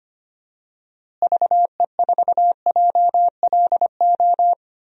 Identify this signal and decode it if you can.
Morse code, heard as a keyed tone, about 25 wpm, 700 Hz. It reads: VE4JLO
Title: DXCC-100 Call Signs - 6 Characters - Repeated 2x - 25wpm